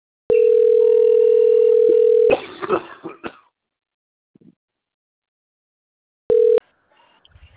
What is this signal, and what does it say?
Ringback tone (North America), 440 + 480 Hz.